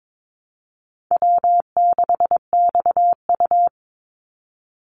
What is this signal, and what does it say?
Morse code, heard as a keyed tone, about 22 wpm, 700 Hz. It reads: W6XU
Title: Easy Call Signs Repeated 3x - 22wpm